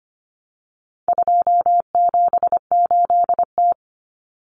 Morse code, heard as a keyed tone, about 25 wpm, 700 Hz. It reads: 278T
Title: Common Cut Numbers - 4-Digit 25wpm